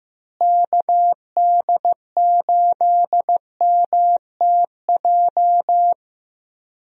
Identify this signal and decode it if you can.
Morse code, heard as a keyed tone, about 15 wpm, 700 Hz. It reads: KD8MTJ
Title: USA Call Signs - 6 Characters - Repeated 3x - 15wpm